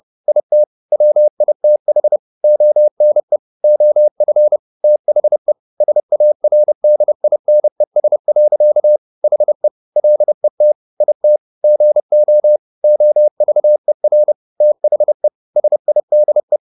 Morse code, heard as a keyed tone, about 30 wpm, 600 Hz. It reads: IT WITH ONE OF THE SARDINES. HE LET IT GO OVER THE SIDE
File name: the old man and the sea / the old man and the sea 1 30wpm